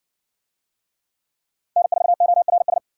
Morse code, heard as a keyed tone, about 52 wpm, 700 Hz. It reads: N4ORS